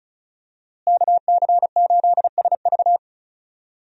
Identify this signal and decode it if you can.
Morse code, heard as a keyed tone, about 36 wpm, 700 Hz. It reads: KC8SV
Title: USA Call Signs - 5 Characters - Repeated 3x - 35wpm